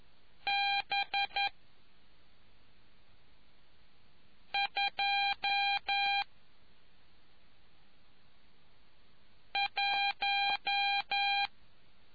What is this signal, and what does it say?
Morse code, heard as a keyed tone, about 11 wpm, 800 Hz. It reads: B 2 1